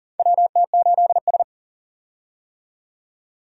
Morse code, heard as a keyed tone, about 40 wpm, 700 Hz. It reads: WT8S
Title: USA Call Signs - 4 Characters - 40wpm